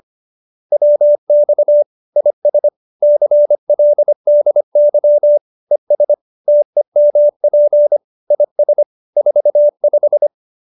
Morse code, heard as a keyed tone, about 25 wpm, 590 Hz. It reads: WX IS CLDY ES TEMP IS 45